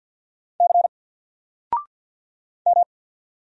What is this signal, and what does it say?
Morse code, heard as a keyed tone, about 50 wpm, 700 Hz. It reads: C M
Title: Single Character (Letter:Number:Punctuation:Prosign) Alt 50wpm